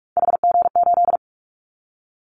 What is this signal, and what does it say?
Morse code, heard as a keyed tone, about 46 wpm, 700 Hz. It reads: HG8
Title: Callsign Prefixes for Top 25 DXCC - 45wpm